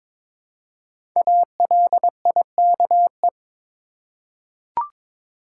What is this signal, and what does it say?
Morse code, heard as a keyed tone, about 22 wpm, 700 Hz. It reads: ALIKE
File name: 5 Letter Words - First 3 Letters Given - 22wpm